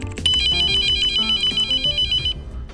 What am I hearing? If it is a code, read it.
There is no code.